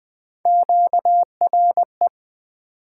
Morse code, heard as a keyed tone, about 20 wpm, 700 Hz. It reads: QRE
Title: Three Letter-Number Combinations - 20wpm